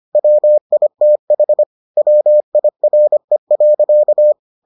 Morse code, heard as a keyed tone, about 25 wpm, 600 Hz. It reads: WITH WIRE.